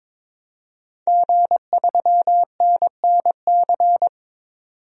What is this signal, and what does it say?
Morse code, heard as a keyed tone, about 22 wpm, 700 Hz. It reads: G3NNC
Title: DXCC-100 Call Signs - 5 Characters - 22wpm